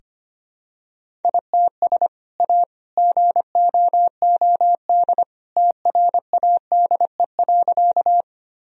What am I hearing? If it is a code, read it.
Morse code, heard as a keyed tone, about 25 wpm, 700 Hz. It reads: ITS A GOOD TRADE.
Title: Sentences from Top 1000 Words - Review Part 4 - 25wpm